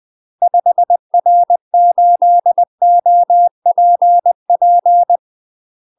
Morse code, heard as a keyed tone, about 20 wpm, 700 Hz. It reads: 5R8OPP
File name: UK MASTER QSOs TRAINING / 20WPM / UK MASTER QSOs 0049